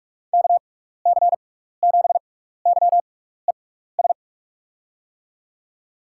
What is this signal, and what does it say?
Morse code, heard as a keyed tone, about 44 wpm, spaced as for 15 wpm, 700 Hz. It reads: KC7YES
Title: USA Call Signs - 6 Characters - 6x Character Spacing - 45wpm